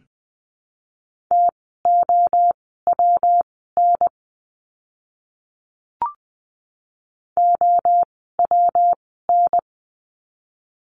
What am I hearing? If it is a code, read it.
Morse code, heard as a keyed tone, about 20 wpm, spaced as for 14 wpm, 700 Hz. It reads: TOWN OWN
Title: 115 Lesson 19f - Letter w - Words 20wpm